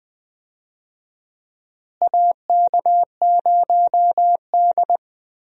Morse code, heard as a keyed tone, about 20 wpm, 700 Hz. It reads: AK0D
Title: USA Call Signs - 4 Characters - Repeated 3x - 20wpm